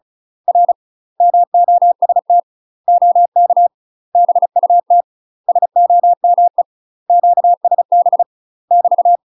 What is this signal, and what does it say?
Morse code, heard as a keyed tone, about 34 wpm, 700 Hz. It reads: R MOST OK BUT SOME QSB <BT>
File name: UK MASTER QSOs TRAINING / 35WPM / UK MASTER QSOs 0030